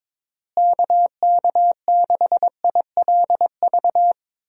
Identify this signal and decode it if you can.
Morse code, heard as a keyed tone, about 22 wpm, 700 Hz. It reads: KK6ILV
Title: USA Call Signs - 6 Characters - Repeated 2x - 22wpm